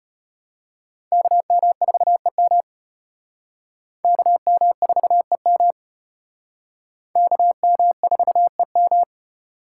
Morse code, heard as a keyed tone, about 34 wpm, 700 Hz. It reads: KM4EM KM4EM KM4EM